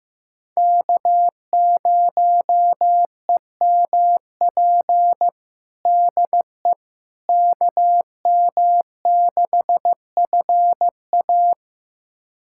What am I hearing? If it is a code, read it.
Morse code, heard as a keyed tone, about 15 wpm, 700 Hz. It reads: K0EMP DE KM6FA
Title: USA Callsign DE Callsign - 5 Characters - 15wpm